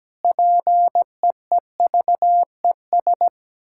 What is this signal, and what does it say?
Morse code, heard as a keyed tone, about 17 wpm, 700 Hz. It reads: PEEVES
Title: Pangram Challenge v2 - 17wpm